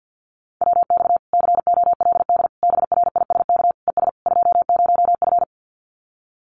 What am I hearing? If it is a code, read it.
Morse code, heard as a keyed tone, about 50 wpm, 700 Hz. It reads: WX COLD BRISK ES 20F
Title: Common QSO Phrases Encore 50wpm